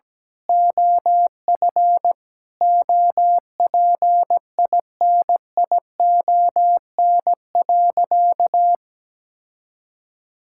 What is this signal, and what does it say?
Morse code, heard as a keyed tone, about 17 wpm, 700 Hz. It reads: OF OPINION.
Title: Sentences from Top 1600 Words 17wpm